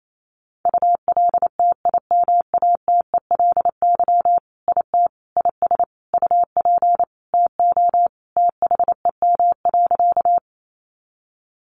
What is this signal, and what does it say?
Morse code, heard as a keyed tone, about 28 wpm, 700 Hz. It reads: ULTIMATELY IT IS UP TO THEM.
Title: Sentences from Top 1200 Words 28wpm